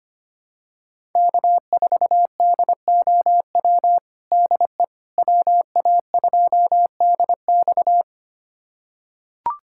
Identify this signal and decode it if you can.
Morse code, heard as a keyed tone, about 25 wpm, 700 Hz. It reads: K4DOW DE WA2DX
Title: USA Callsign DE Callsign - 5 Characters - 25wpm